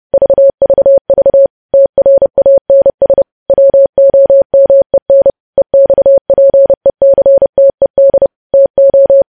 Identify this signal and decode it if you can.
Morse code, heard as a keyed tone, about 30 wpm, 550 Hz. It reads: VVV TRANS WOMEN EXPECTED TO
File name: podcast / News headlines-1x-30x30